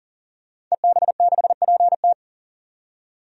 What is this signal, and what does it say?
Morse code, heard as a keyed tone, about 40 wpm, 700 Hz. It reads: EB6PT
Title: DXCC-100 Call Signs - 5 Characters - 40wpm